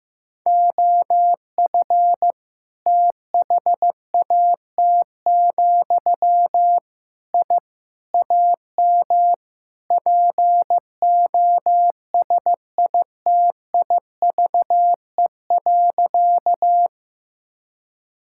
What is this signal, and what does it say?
Morse code, heard as a keyed tone, about 15 wpm, 700 Hz. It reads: OF THAT, I AM POSITIVE.
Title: Sentences from Top 1000 Words - Review Part 5 - 15wpm